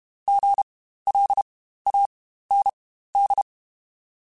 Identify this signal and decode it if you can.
Morse code, heard as a keyed tone, about 32 wpm, spaced as for 15 wpm, 800 Hz. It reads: GLAND
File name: Spaced Alphas / Spaced Alphas mp3s / Gs sp @30